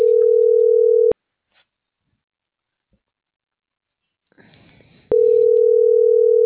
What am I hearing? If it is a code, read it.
Ringback tone (North America), 440 + 480 Hz.